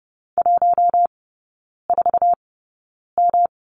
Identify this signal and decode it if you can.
Morse code, heard as a keyed tone, about 30 wpm, spaced as for 9 wpm, 700 Hz. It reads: 14M
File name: Lesson 18e - Letter m - Three Characters 30wpm